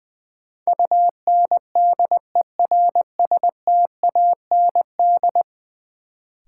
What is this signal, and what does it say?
Morse code, heard as a keyed tone, about 20 wpm, 700 Hz. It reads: UNDERSTAND
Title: Sets of 4 Words - New Word Drill - 20wpm